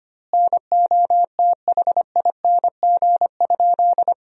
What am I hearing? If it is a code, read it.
Morse code, heard as a keyed tone, about 25 wpm, 700 Hz. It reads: NOTHING?
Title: 2x Word-Spacing - Sentences from Top 500 Words - 25wpm